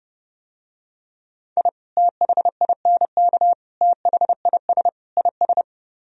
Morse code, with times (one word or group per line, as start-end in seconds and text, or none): 1.6-1.7 I
2.0-3.5 THINK
3.8-4.9 THIS
5.2-5.6 IS